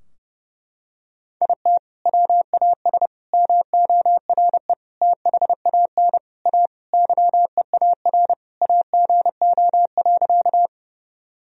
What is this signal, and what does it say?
Morse code, heard as a keyed tone, about 32 wpm, 700 Hz. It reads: IT WAS MORE THAN A YEAR AGO.
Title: Sentences from Top 1000 Words - Review Part 3 - 30wpm